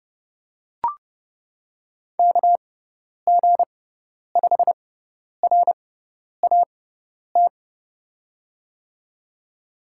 Morse code, heard as a keyed tone, about 30 wpm, spaced as for 10 wpm, 700 Hz. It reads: KG5RAT
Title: USA Call Signs - 6 Characters - 6x Character Spacing - 30wpm